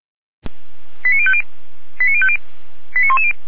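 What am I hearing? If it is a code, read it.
Five-tone selective call (ZVEI), heard as quick runs of five tones, tones of about 70 ms. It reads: 89050 89050 89110
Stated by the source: ZVEI five-tone selective call received on SDR.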